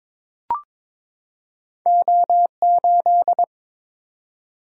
Morse code, heard as a keyed tone, about 22 wpm, 700 Hz. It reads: O8
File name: practice / 22wpm / Two Letter-Number Combinations - 22wpm